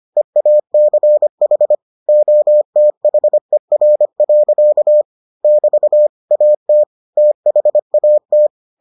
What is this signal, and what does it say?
Morse code, heard as a keyed tone, about 25 wpm, 600 Hz. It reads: EACH OTHER. <BT> AT THAT